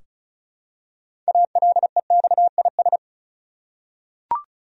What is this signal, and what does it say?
Morse code, heard as a keyed tone, about 36 wpm, 700 Hz. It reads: ALEXIS
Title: USA Common Names - 35wpm